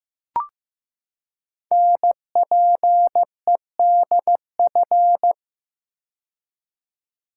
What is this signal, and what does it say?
Morse code, heard as a keyed tone, about 15 wpm, 700 Hz. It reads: NPEDF
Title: Code Groups - 5 Letter:Numbers - 15wpm